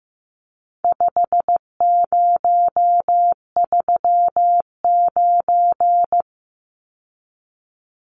Morse code, heard as a keyed tone, about 14 wpm, 700 Hz. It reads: 5039
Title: Numbers - 4-Digit 15wpm